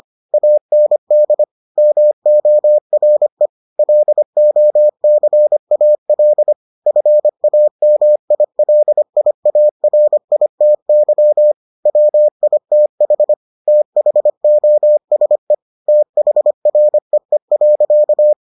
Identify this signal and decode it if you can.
Morse code, heard as a keyed tone, about 25 wpm, 600 Hz. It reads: AND MORE LOCAL FAMILIARITY WITH THOSE THREE.